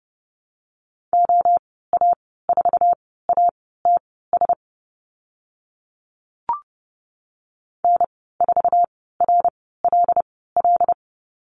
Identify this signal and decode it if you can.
Morse code, heard as a keyed tone, about 30 wpm, spaced as for 17 wpm, 700 Hz. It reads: OA4ATS N4RLL
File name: Lesson 15g - Number 5 - Call Signs 30wpm